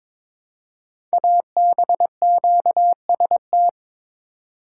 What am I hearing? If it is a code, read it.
Morse code, heard as a keyed tone, about 23 wpm, 700 Hz. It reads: ABQST
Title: Code Groups - 5 Letters - 22wpm